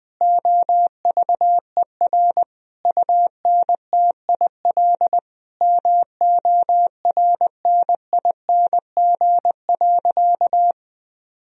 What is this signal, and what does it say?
Morse code, heard as a keyed tone, about 20 wpm, 700 Hz. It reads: OVER UNTIL MORNING.
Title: Sentences from Top 1000 Words - Review Part 1 - 20wpm